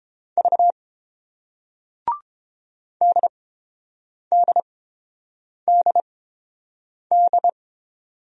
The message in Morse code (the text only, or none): V DDDD